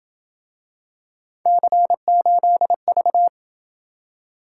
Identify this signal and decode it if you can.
Morse code, heard as a keyed tone, about 27 wpm, 700 Hz. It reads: C8V